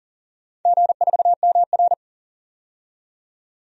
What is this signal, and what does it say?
Morse code, heard as a keyed tone, about 42 wpm, 700 Hz. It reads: G4MR
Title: DXCC-100 Call Signs - 4 Characters - 40wpm